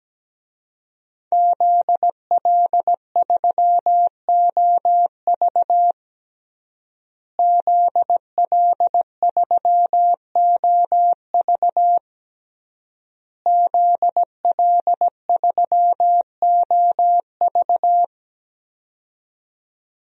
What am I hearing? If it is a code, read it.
Morse code, heard as a keyed tone, about 17 wpm, 700 Hz. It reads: ZL3OV ZL3OV ZL3OV
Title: DXCC-100 Call Signs - 5 Characters - Repeated 3x - 17wpm